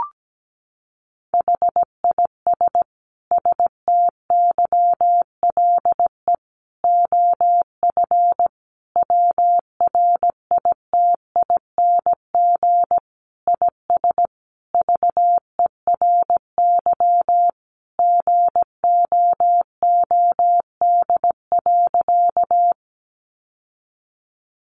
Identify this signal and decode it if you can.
Morse code, heard as a keyed tone, about 17 wpm, 700 Hz. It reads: HIS STYLE OF WRITING IS VERY GOOD.
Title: Sentences from Top 800 Words 17wpm